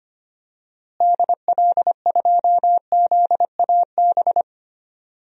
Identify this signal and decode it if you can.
Morse code, heard as a keyed tone, about 25 wpm, 700 Hz. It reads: DL2ZAB
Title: DXCC-100 Call Signs - 6 Characters - Repeated 3x - 25wpm